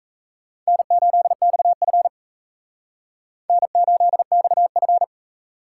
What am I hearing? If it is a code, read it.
Morse code, heard as a keyed tone, about 35 wpm, 700 Hz. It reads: N8XF N8XF
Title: Speed Racing - USA Call Signs 28wpm